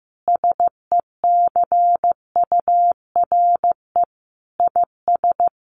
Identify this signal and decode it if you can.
Morse code, heard as a keyed tone, about 15 wpm, 700 Hz. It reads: SECURE IS
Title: Sentences from Top 1200 Words 15wpm